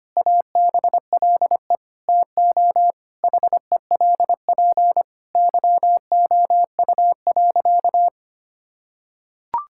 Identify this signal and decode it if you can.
Morse code, heard as a keyed tone, about 25 wpm, 700 Hz. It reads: ABLE TO HELP YOU.
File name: practice / 25wpm / Sentences with 4-Word Phrases - 25wpm